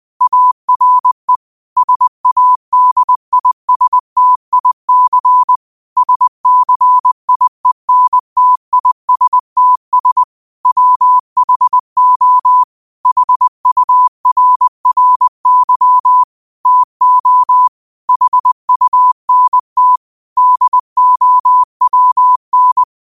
Morse code, heard as a keyed tone, about 20 wpm, 1000 Hz. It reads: ARE SADISTIC SCIENTISTS WHO HURRY TO HUNT DOWN
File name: qotdfiles / 20wpm